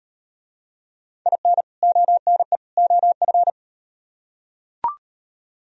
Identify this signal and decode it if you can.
Morse code, heard as a keyed tone, about 36 wpm, 700 Hz. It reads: IN ONE OF